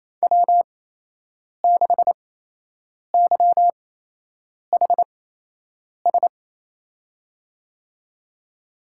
Morse code, heard as a keyed tone, about 28 wpm, spaced as for 8 wpm, 700 Hz. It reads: W6YHS
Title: USA Call Signs - 5 Characters - 8x Character Spacing - 28wpm